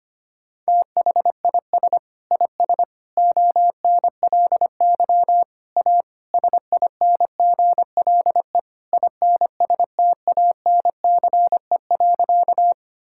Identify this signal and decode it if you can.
Morse code, heard as a keyed tone, about 25 wpm, 700 Hz. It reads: THIS IS ONLY A SINGLE INSTANCE.
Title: Sentences from Top 1000 Words - Review Part 5 - 25wpm